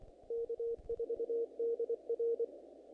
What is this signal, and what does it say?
Morse code, heard as a keyed tone, about 25 wpm, 460 Hz. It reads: K4DR
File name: contests / naqp-cw-k5tr-op-k5pi- -1900